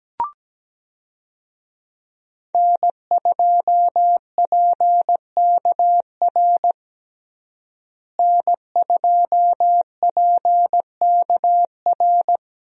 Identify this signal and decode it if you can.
Morse code, heard as a keyed tone, about 17 wpm, 700 Hz. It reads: N2PKR N2PKR
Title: USA Call Signs - 5 Characters - Repeated 3x - 17wpm